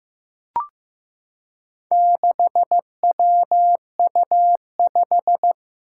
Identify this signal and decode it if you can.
Morse code, heard as a keyed tone, about 15 wpm, 700 Hz. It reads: 6WU5
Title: Four Challenging Letter-Number Combinations 15wpm